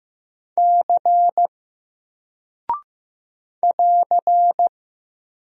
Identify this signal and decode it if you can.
Morse code, heard as a keyed tone, about 15 wpm, 700 Hz. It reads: C <AR>
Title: Single Character (Letter:Number:Punctuation:Prosign) Alt 15wpm